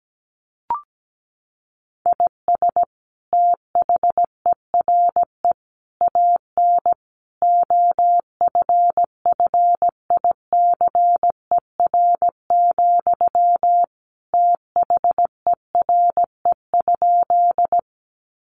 Morse code, heard as a keyed tone, about 17 wpm, 700 Hz. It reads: IS THERE AN OFFICER, THERE?
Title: Sentences from Top 1200 Words 17wpm